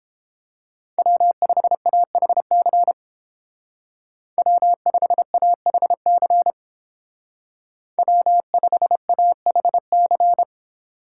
Morse code, heard as a keyed tone, about 30 wpm, 700 Hz. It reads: W5AHC W5AHC W5AHC